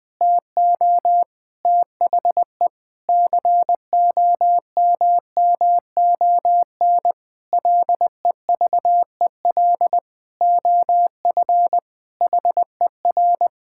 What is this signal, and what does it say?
Morse code, heard as a keyed tone, about 20 wpm, 700 Hz. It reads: TO THE COMMON LEVEL OF HER
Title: Sentences from Top 1600 Words 20wpm